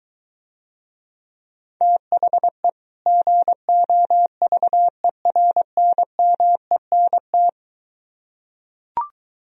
Morse code, heard as a keyed tone, about 23 wpm, 700 Hz. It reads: THE GOVERNMENT